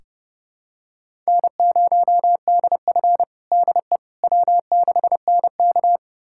Morse code, heard as a keyed tone, about 32 wpm, 700 Hz. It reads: N0DF DE W6NK
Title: USA Callsign DE Callsign - 4 Characters - 30wpm